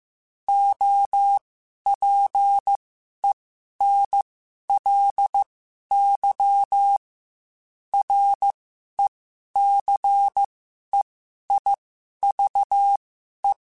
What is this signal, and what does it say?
Morse code, heard as a keyed tone, about 15 wpm, spaced as for 11 wpm, 800 Hz. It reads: OPENLY RECEIVE